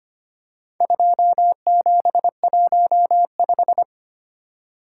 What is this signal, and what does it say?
Morse code, heard as a keyed tone, about 25 wpm, 700 Hz. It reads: 2715
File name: practice / 25wpm / Four Letter-Number Combinations - 25wpm